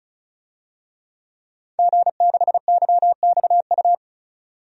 Morse code, heard as a keyed tone, about 34 wpm, 700 Hz. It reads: G6YXU
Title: DXCC-100 Call Signs - 5 Characters - 35wpm